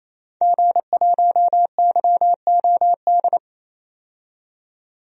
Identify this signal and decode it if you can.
Morse code, heard as a keyed tone, about 28 wpm, 700 Hz. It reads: G1YOD
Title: DXCC-100 Call Signs - 5 Characters - 28wpm